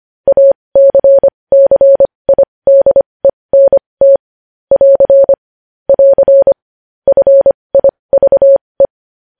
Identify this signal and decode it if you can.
Morse code, heard as a keyed tone, about 25 wpm, spaced as for 20 wpm, 550 Hz. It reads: ACCIDENT <AR> <AR> FIVE
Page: News Headlines in Morse Code at 20 WPM